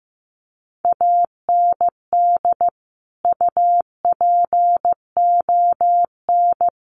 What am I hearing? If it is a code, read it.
Morse code, heard as a keyed tone, about 15 wpm, 700 Hz. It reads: AND UPON
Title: Sentences from Top 1100 Words 15wpm